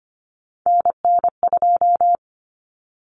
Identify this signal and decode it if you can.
Morse code, heard as a keyed tone, about 25 wpm, 700 Hz. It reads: NN2
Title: Common Cut Numbers - 3-Digit 25wpm